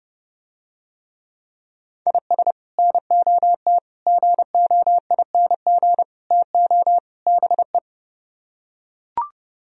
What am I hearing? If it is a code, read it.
Morse code, heard as a keyed tone, about 30 wpm, 700 Hz. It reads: IS NOT GOING TO BE